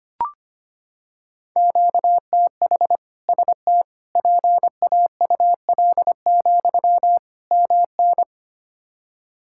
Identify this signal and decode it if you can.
Morse code, heard as a keyed tone, about 25 wpm, 700 Hz. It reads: QTH ST PAUL, MN
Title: K1USN SST QSO Practice 25wpm